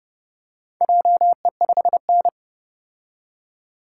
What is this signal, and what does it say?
Morse code, heard as a keyed tone, about 30 wpm, 700 Hz. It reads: JE5N